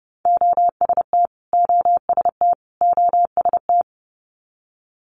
Morse code, heard as a keyed tone, about 32 wpm, 700 Hz. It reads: OST OST OST
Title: Multi-letter Phonemes Repeated 3x 30wpm